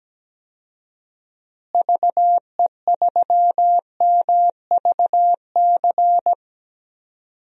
Morse code, heard as a keyed tone, about 17 wpm, 700 Hz. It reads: VE3MVC
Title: DXCC-100 Call Signs - 6 Characters - Repeated 3x - 17wpm